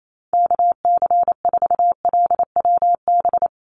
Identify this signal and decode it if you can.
Morse code, heard as a keyed tone, about 28 wpm, 700 Hz. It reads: KC4LWB
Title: USA Call Signs 28wpm